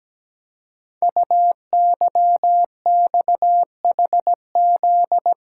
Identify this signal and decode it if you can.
Morse code, heard as a keyed tone, about 17 wpm, 700 Hz. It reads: UYXHZ
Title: Code Groups - 5 Letters - 17wpm